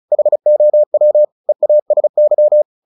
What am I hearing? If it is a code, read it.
Morse code, heard as a keyed tone, about 34 wpm, 600 Hz. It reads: HOW EASY
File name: the old man and the sea 2 35wpm